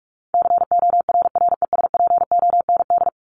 Morse code, heard as a keyed tone, about 44 wpm, 700 Hz. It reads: CORRESPOND
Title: ING Suffix - Top 100 Words with Base Words 45wpm